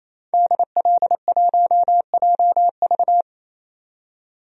Morse code, heard as a keyed tone, about 28 wpm, 700 Hz. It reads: DL1JV
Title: DXCC-100 Call Signs - 5 Characters - 28wpm